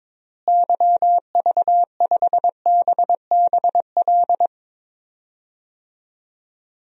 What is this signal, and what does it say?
Morse code, heard as a keyed tone, about 22 wpm, 700 Hz. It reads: YV5BBL
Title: DXCC-100 Call Signs - 6 Characters - Repeated 3x - 22wpm